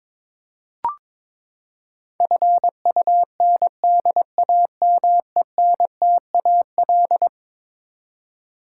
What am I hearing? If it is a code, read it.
Morse code, heard as a keyed tone, about 22 wpm, 700 Hz. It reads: FUNDAMENTAL
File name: practice / 22wpm / Eleven Letter Words - 22wpm